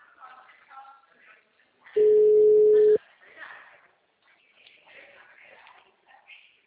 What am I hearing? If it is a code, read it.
Ringback tone (Europe, 425 Hz), with speech.
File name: out- -2011- - - .14